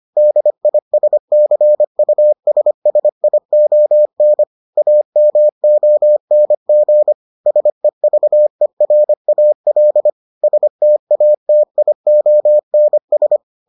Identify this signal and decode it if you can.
Morse code, heard as a keyed tone, about 25 wpm, 600 Hz. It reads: DISCUSSION AMONG SEVERAL STATIONS